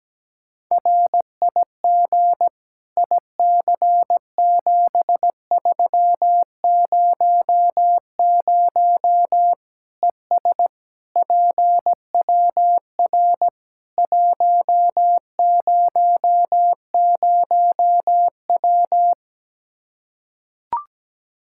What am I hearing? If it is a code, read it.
Morse code, heard as a keyed tone, about 17 wpm, 700 Hz. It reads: RIG IC7300 ES PWR 100W
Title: Practice QSO - Exchange 2 - Info - 17wpm